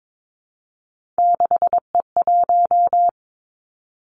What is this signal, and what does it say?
Morse code, heard as a keyed tone, about 22 wpm, 700 Hz. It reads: 6E1
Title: Three Challenging Letter-Number Combinations 22wpm